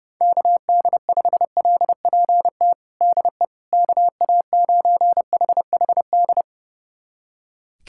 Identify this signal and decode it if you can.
Morse code, heard as a keyed tone, about 30 wpm, 700 Hz. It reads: KD5LPT DE KA9HHD
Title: USA Callsign DE Callsign 30wpm